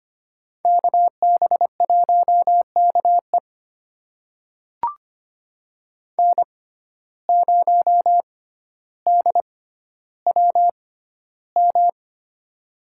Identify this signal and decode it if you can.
Morse code, heard as a keyed tone, about 25 wpm, spaced as for 13 wpm, 700 Hz. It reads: KB1KE N0DWM
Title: USA Call Signs - 5 Characters - 6x Character Spacing - 25wpm